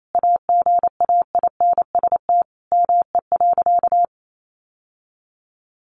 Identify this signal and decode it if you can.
Morse code, heard as a keyed tone, about 28 wpm, 700 Hz. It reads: AGAINST ME.
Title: Sentences from Top 1100 Words 28wpm